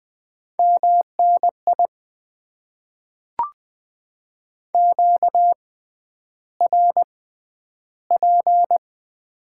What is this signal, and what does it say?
Morse code, heard as a keyed tone, about 20 wpm, spaced as for 10 wpm, 700 Hz. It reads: MNI QRP
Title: Top 90 QSO Elements - 6x Character Spacing - 20wpm